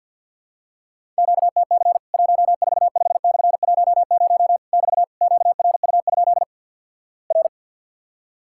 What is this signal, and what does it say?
Morse code, heard as a keyed tone, about 50 wpm, 700 Hz. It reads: QTC 145/10 <BT> QRU? R
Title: Worked All Europe DX Contest - QTC Exchange with Repeats - 50wpm